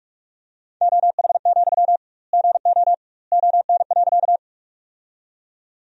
Morse code, heard as a keyed tone, about 46 wpm, 700 Hz. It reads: OH, GO ON.
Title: 2x Word-Spacing - Sentences from Top 500 Words - 45wpm